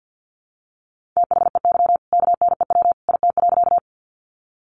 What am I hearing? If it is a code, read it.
Morse code, heard as a keyed tone, about 48 wpm, 700 Hz. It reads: THEY KNEW IT.
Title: Sentences from Top 400 Words 50wpm